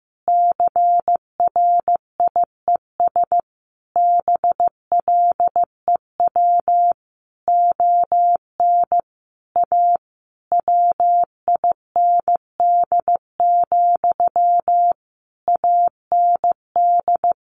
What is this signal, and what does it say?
Morse code, heard as a keyed tone, about 15 wpm, 700 Hz. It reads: CRIES BLEW ON A WIND, AND
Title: A Sound Of Thunder 15wpm